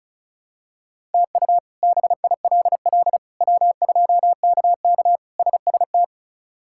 Morse code, heard as a keyed tone, about 36 wpm, 700 Hz. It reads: TU BILL W2KK SST
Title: K1USN SST QSO Practice - 35wpm